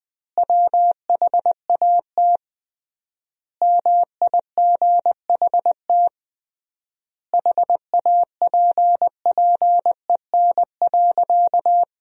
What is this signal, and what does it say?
Morse code, heard as a keyed tone, about 20 wpm, 700 Hz. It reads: WHAT MIGHT HAPPEN.
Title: 3x Word-Spacing - Sentences from Top 400 Words - 20wpm